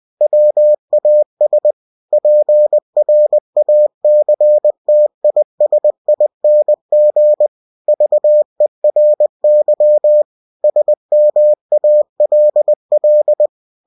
Morse code, heard as a keyed tone, about 20 wpm, 600 Hz. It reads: WAS PRACTISING VERY SMALL
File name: cw-books / winnie-the-pooh / winnie-20wpm-page-0066